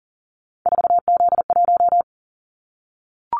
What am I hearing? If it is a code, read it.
Morse code, heard as a keyed tone, about 40 wpm, 700 Hz. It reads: 4Z1
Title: Callsign Prefixes for Top 50 DXCC - 40wpm